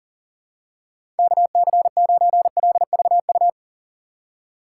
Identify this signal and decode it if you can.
Morse code, heard as a keyed tone, about 40 wpm, 700 Hz. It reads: KC9LVU
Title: USA Call Signs - 6 Characters - 2x Character Spacing - 40wpm